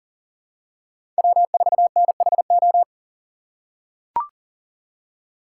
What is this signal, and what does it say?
Morse code, heard as a keyed tone, about 40 wpm, 700 Hz. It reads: W4NHO
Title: USA Call Signs - 6x Character Spacing - 40wpm